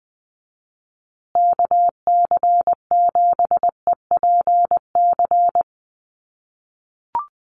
Morse code, heard as a keyed tone, about 20 wpm, 700 Hz. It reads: KC7EPC
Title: USA Call Signs Repeated 3x 20wpm